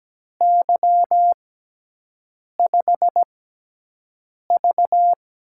Morse code, heard as a keyed tone, about 17 wpm, spaced as for 6 wpm, 700 Hz. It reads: Y5V
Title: USA Call Signs - 6x Character Spacing - 17wpm